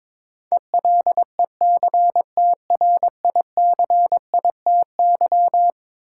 Morse code, heard as a keyed tone, about 22 wpm, 700 Hz. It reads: ELECTRICITY
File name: practice / 22wpm / Eleven Letter Words - 22wpm